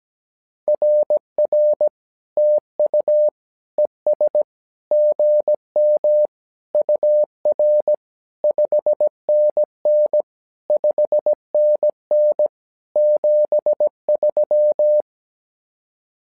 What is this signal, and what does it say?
Morse code, heard as a keyed tone, about 17 wpm, 600 Hz. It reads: RR TU ES GM UR 5NN 5NN 73
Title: SOTA QSO Practice 17wpm